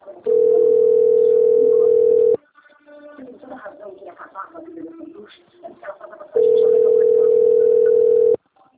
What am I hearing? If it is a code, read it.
Ringback tone (North America), 440 + 480 Hz, with speech.
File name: rg-1600-unknown- - - .41777